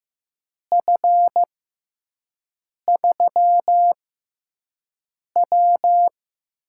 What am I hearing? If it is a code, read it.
Morse code, heard as a keyed tone, about 15 wpm, spaced as for 5 wpm, 700 Hz. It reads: F3W
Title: USA Call Signs - 4 Characters - 6x Character Spacing - 15wpm